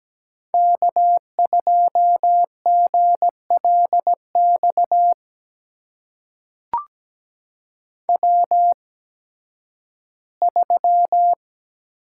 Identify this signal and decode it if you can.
Morse code, heard as a keyed tone, about 17 wpm, spaced as for 11 wpm, 700 Hz. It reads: K2GLX W3
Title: USA Call Signs - 5 Characters - 8x Character Spacing - 17wpm